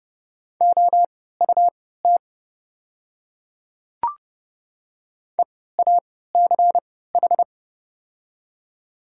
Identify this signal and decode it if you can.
Morse code, heard as a keyed tone, about 30 wpm, spaced as for 17 wpm, 700 Hz. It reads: OUT EACH
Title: 3x Character Spacing - Top 300 Words - 30wpm